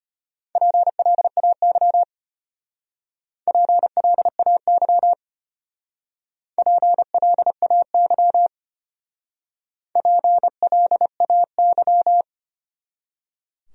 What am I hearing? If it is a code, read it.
Morse code, heard as a keyed tone, about 31 wpm, 700 Hz. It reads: PLAY PLAY PLAY PLAY
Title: Speed-Racing Top 500 Words 25wpm